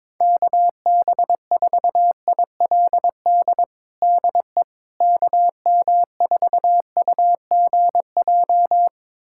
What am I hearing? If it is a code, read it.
Morse code, heard as a keyed tone, about 22 wpm, 700 Hz. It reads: KB4ILD DE KM4UGJ
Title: USA Callsign DE Callsign - 6 Characters - 22wpm